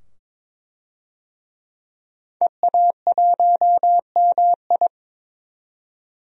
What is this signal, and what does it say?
Morse code, heard as a keyed tone, about 22 wpm, 700 Hz. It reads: EA1MI
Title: DXCC-100 Call Signs - 5 Characters - 22wpm